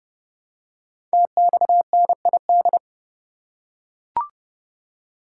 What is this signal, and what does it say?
Morse code, heard as a keyed tone, about 29 wpm, 700 Hz. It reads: TXNID
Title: Code Groups - 5 Letter:Numbers - 30wpm